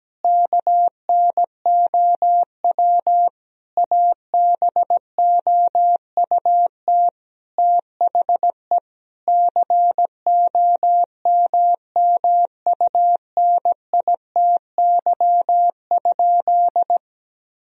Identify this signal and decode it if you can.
Morse code, heard as a keyed tone, about 17 wpm, 700 Hz. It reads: KNOW ABOUT THE COMMUNITY?
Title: Sentences from Top 1000 Words - Review Part 4 - 17wpm